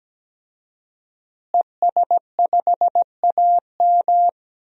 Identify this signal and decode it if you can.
Morse code, heard as a keyed tone, about 17 wpm, 700 Hz. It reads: ES5AM
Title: DXCC-100 Call Signs - 5 Characters - Repeated 3x - 17wpm